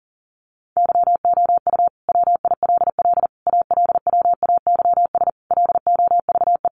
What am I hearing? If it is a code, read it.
Morse code, heard as a keyed tone, about 40 wpm, 700 Hz. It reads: YOU WILL ALWAYS LOVE
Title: Sentences from Top 400 Words 40wpm